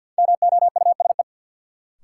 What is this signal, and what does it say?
Morse code, heard as a keyed tone, about 52 wpm, 700 Hz. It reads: MOUSE
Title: Sentences from Top 700 Words - New Word Drill - 50wpm